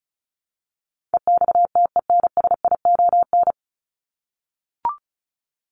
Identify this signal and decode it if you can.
Morse code, heard as a keyed tone, about 34 wpm, 700 Hz. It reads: EXTENSION
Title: Top 1501-1750 Words 35wpm